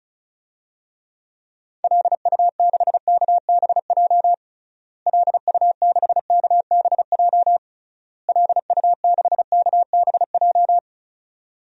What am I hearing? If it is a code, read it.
Morse code, heard as a keyed tone, about 34 wpm, 700 Hz. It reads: LU6KBJ LU6KBJ LU6KBJ
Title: DXCC-100 Call Signs - 6 Characters - Repeated 3x - 35wpm